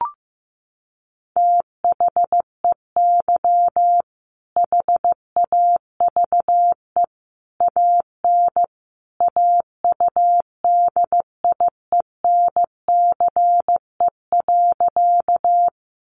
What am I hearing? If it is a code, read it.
Morse code, heard as a keyed tone, about 14 wpm, 700 Hz. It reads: THEY HAVE AN AUDIENCE.
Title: Sentences from Top 800 Words 15wpm